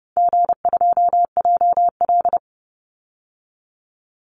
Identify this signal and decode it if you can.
Morse code, heard as a keyed tone, about 29 wpm, 700 Hz. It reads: G2JL
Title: Call Signs - 30wpm